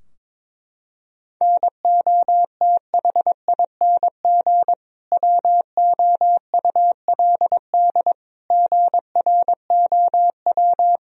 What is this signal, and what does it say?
Morse code, heard as a keyed tone, about 22 wpm, 700 Hz. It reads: NOTHING WOULD GROW
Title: Sentences from Top 1000 Words - Review Part 3 - 22wpm